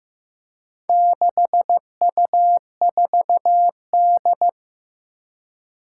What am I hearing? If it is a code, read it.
Morse code, heard as a keyed tone, about 15 wpm, 700 Hz. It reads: 6U4D
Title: Four Challenging Letter-Number Combinations 15wpm